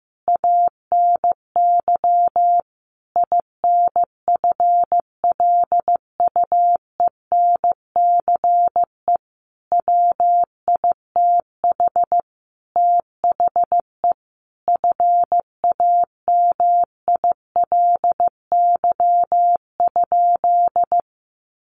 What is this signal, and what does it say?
Morse code, heard as a keyed tone, about 15 wpm, 700 Hz. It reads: ANY INFLUENCE WITH THE FAMILY?
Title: Sentences from Top 1100 Words 15wpm